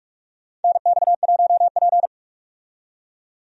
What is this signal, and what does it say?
Morse code, heard as a keyed tone, about 46 wpm, 700 Hz. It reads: NX1P
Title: USA Call Signs - 4 Characters - 45wpm